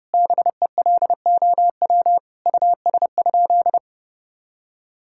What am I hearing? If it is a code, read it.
Morse code, heard as a keyed tone, about 30 wpm, 700 Hz. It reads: BELOW US?
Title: Sentences from Top 1000 Words - Review Part 5 - 30wpm